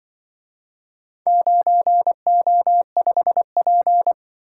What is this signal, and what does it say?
Morse code, heard as a keyed tone, about 25 wpm, 700 Hz. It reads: 9O5P
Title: Speed-Racing - Four Letter-Number Combinations - 20wpm